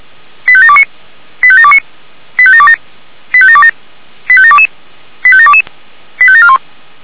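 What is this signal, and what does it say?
Five-tone selective call (ZVEI), heard as quick runs of five tones, tones of about 70 ms. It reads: 85729 85729 85727 85727 85722 85722 85741
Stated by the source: ZVEI five-tone selective call received by SDR, from the feuerwehr-orlamuende.org fire brigade audio server.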